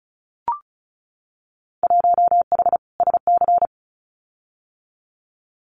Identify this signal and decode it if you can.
Morse code, heard as a keyed tone, about 36 wpm, 690 Hz. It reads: 1H SC
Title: Winter Field Day Exchanges 35wpm